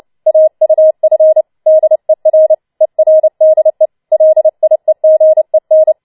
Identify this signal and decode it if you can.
Morse code, heard as a keyed tone, about 29 wpm, 630 Hz. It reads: AUF DER ERDE LIEGEN